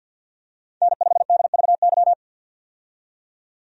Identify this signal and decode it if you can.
Morse code, heard as a keyed tone, about 51 wpm, 700 Hz. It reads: N5DVY
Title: USA Call Signs - 5 Characters - 50wpm